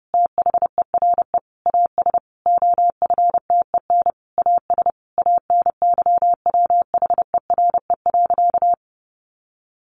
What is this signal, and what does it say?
Morse code, heard as a keyed tone, about 30 wpm, 700 Hz. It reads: THERE AS OFTEN AS ANYWHERE.
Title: Sentences from Top 1400 Words 30wpm